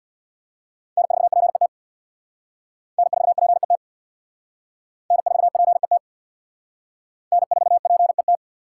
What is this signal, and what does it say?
Morse code, heard as a keyed tone, about 59 wpm, 700 Hz. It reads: N4PET N4PET N4PET N4PET
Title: Speed Racing - USA 1x3 Call Signs 50wpm